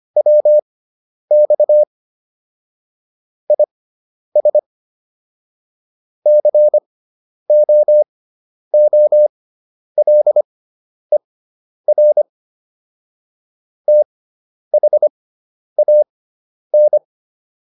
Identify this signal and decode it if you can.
Morse code, heard as a keyed tone, about 25 wpm, spaced as for 10 wpm, 600 Hz. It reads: WX IS COOLER THAN
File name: QSO409 10